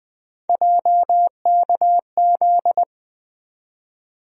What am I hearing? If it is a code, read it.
Morse code, heard as a keyed tone, about 20 wpm, 700 Hz. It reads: JKZ
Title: USA Call Sign Postfixes - 3 Letters 20wpm